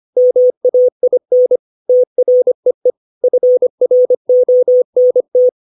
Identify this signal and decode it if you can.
Morse code, heard as a keyed tone, about 25 wpm, 500 Hz. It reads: MAIN TREE FRONT
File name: 632 words / 632 Words 25WPM